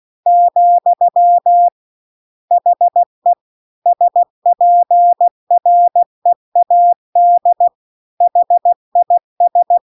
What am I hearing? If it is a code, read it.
Morse code, heard as a keyed tone, about 16 wpm, 700 Hz. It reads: , HE SPREAD HIS